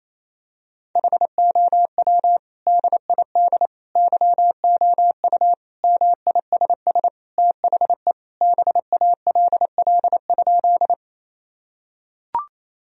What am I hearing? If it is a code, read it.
Morse code, heard as a keyed tone, about 28 wpm, 700 Hz. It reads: HOW DID YOU MISS THE BALL?
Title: Sentences from Top 1000 Words - Review Part 3 - 28wpm